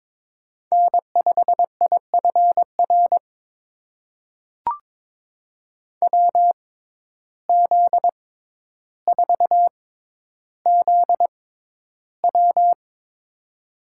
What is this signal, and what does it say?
Morse code, heard as a keyed tone, about 22 wpm, spaced as for 11 wpm, 700 Hz. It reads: N5IFR WZ4ZW
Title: USA Call Signs - 5 Characters - 6x Character Spacing - 22wpm